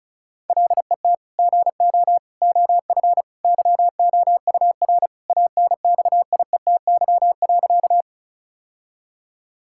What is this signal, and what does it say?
Morse code, heard as a keyed tone, about 34 wpm, 700 Hz. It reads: LET GO OF YOUR ANXIETY.
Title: Sentences from Top 1600 Words 35wpm